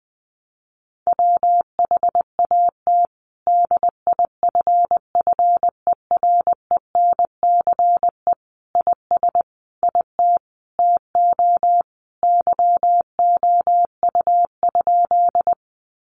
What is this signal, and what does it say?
Morse code, heard as a keyed tone, about 20 wpm, 700 Hz. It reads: WHAT DIFFERENCE IS IT TO YOU?
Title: Sentences from Top 800 Words 20wpm